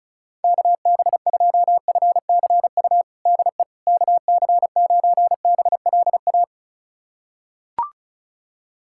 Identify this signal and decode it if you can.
Morse code, heard as a keyed tone, about 36 wpm, 700 Hz. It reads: KB2FCU DE KC9BLA
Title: USA Callsign DE Callsign 35wpm